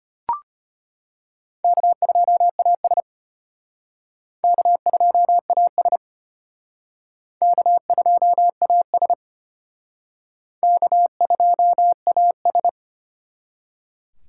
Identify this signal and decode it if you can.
Morse code, heard as a keyed tone, about 32 wpm, 700 Hz. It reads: K2AS K2AS K2AS K2AS